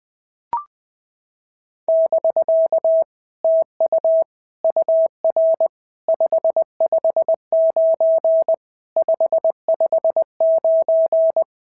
Morse code, heard as a keyed tone, about 20 wpm, 650 Hz. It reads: <BK> TU UR 559 559
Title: POTA QSO Practice - 20wpm